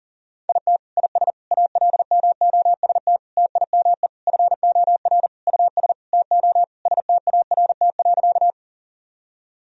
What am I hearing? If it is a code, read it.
Morse code, heard as a keyed tone, about 37 wpm, 700 Hz. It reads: IT IS ALMOST TIME FOR US TO START.
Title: Sentences from Top 1000 Words - Review Part 4 - 40wpm